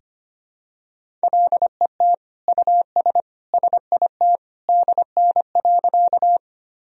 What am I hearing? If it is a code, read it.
Morse code, heard as a keyed tone, about 25 wpm, 700 Hz. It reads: LET US SIT DN.
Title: Sentences from Top 500 Words and Abbreviations 25wpm